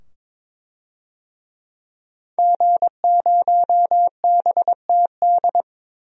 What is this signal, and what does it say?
Morse code, heard as a keyed tone, about 21 wpm, 700 Hz. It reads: G0BTD
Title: DXCC-100 Call Signs - 5 Characters - 22wpm